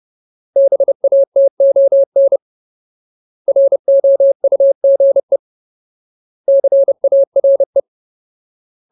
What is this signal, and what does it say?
Morse code, heard as a keyed tone, about 30 wpm, 550 Hz. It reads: BATON ROUGE CARE
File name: Ultimate Word List / 30WPM Ultimate